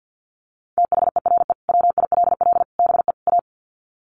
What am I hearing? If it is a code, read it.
Morse code, heard as a keyed tone, about 49 wpm, 700 Hz. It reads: THERE WILL BE A
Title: Sets of 4 Words 50wpm